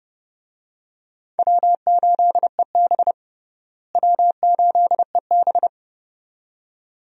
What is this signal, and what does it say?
Morse code, heard as a keyed tone, about 30 wpm, 700 Hz. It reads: W8EB W8EB
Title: USA Call Signs - 4 Characters - Repeated 2x - 30wpm